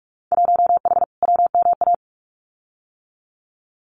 Morse code, heard as a keyed tone, about 44 wpm, 700 Hz. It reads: 1H WMA
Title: Winter Field Day Exchanges 45wpm